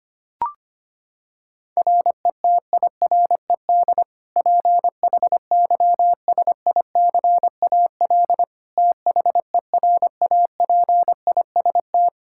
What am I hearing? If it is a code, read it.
Morse code, heard as a keyed tone, about 25 wpm, 700 Hz. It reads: RETIRED PHYSICAL THERAPIST
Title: Practice QSO - Exchange 3 - More Info - 25wpm